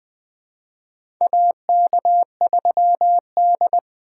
Morse code, heard as a keyed tone, about 20 wpm, 700 Hz. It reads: AK3D
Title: USA Call Signs - 4 Characters - 2x Character Spacing - 20wpm